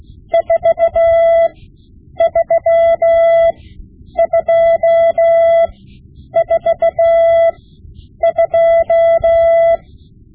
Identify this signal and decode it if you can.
Morse code, heard as a keyed tone, about 16 wpm, spaced as for 9 wpm, 670 Hz. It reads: 43242
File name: Morse Lessons / Morse Lessons / Tape-5B